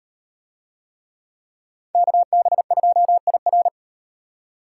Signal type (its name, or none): Morse code